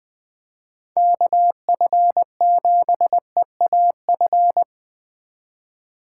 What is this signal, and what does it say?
Morse code, heard as a keyed tone, about 20 wpm, 700 Hz. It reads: KF7EAF
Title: USA Call Signs - 4x Character Spacing - 20wpm